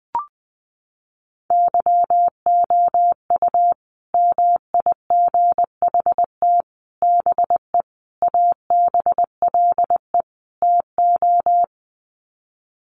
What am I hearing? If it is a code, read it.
Morse code, heard as a keyed tone, about 20 wpm, 700 Hz. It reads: YOU MIGHT BE ABLE TO
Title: Sets of 5 Words Encore2 20wpm